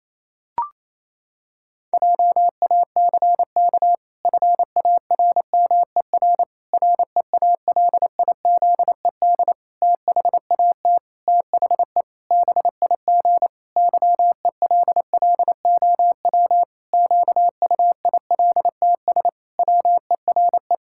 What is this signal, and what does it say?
Morse code, heard as a keyed tone, about 28 wpm, 700 Hz. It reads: JACK FARMER REALIZED THAT THE BIG YELLOW QUILTS WERE
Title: Pangram Challenge v2 - 28wpm